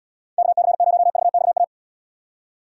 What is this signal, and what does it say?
Morse code, heard as a keyed tone, about 78 wpm, 700 Hz. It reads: KC0VXA